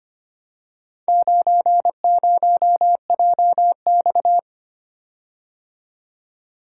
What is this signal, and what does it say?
Morse code, heard as a keyed tone, about 25 wpm, 700 Hz. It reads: 90JX
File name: practice / 25wpm / Four Letter-Number Combinations - 25wpm